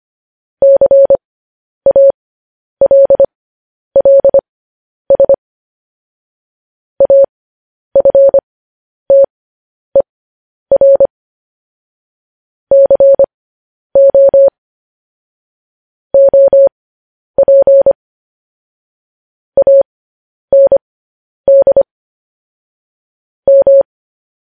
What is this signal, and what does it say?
Morse code, heard as a keyed tone, about 25 wpm, spaced as for 10 wpm, 550 Hz. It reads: CALLS AFTER CO OP AND M
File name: podcast / News headlines-1x-25x10